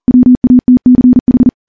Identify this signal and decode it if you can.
Morse code, heard as a keyed tone, about 42 wpm, 250 Hz. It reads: WATCH
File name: morse / morse-training-40WPM